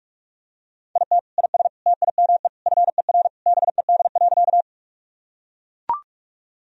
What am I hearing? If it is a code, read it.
Morse code, heard as a keyed tone, about 42 wpm, 690 Hz. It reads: IT IS TIME FER BED.
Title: Sentences from Top 500 Words and Abbreviations 45wpm